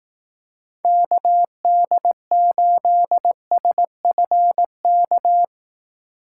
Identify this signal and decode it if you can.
Morse code, heard as a keyed tone, about 19 wpm, 700 Hz. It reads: KD8SFK
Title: Speed Racing - USA 2x3 Call Signs 15wpm